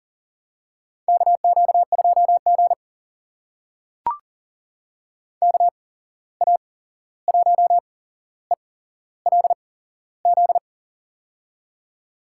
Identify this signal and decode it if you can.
Morse code, heard as a keyed tone, about 39 wpm, spaced as for 15 wpm, 700 Hz. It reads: KQ2G KA1ELZ
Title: USA Call Signs - 8x Character Spacing - 40wpm